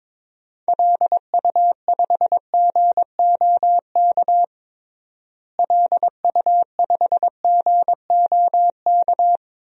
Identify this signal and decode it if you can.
Morse code, heard as a keyed tone, about 22 wpm, 700 Hz. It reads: LU5GOK LU5GOK
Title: DXCC-100 Call Signs - 6 Characters - Repeated 3x - 22wpm